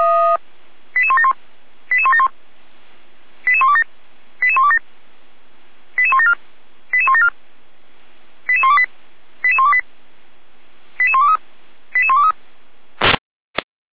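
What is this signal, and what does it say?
Five-tone selective call (ZVEI), heard as quick runs of five tones, tones of about 70 ms. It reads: 88171 88171 88127 88127 88174 88174 88128 88128 88123 88123
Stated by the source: ZVEI five-tone call received by SDR.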